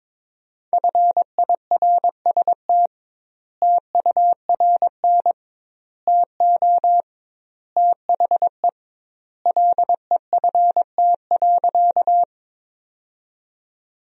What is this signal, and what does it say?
Morse code, heard as a keyed tone, about 21 wpm, 700 Hz. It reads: FIRST TURN TO THE LEFT.
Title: 2x Word-Spacing - Sentences from Top 300 Words - 22wpm